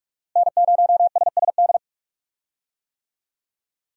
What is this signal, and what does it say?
Morse code, heard as a keyed tone, about 42 wpm, 700 Hz. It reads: N0SSD
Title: USA Call Signs - 5 Characters - Repeated 3x - 45wpm